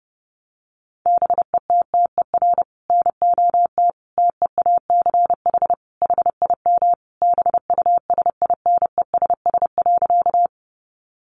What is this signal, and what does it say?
Morse code, heard as a keyed tone, about 28 wpm, 700 Hz. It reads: BETTER NOT TEACH HIM BUSINESS.
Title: Sentences from Top 600 Words 30wpm